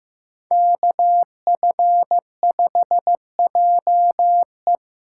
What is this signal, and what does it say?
Morse code, heard as a keyed tone, about 15 wpm, 700 Hz. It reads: KF5JE
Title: USA Call Signs - 5 Characters - 6x Character Spacing - 15wpm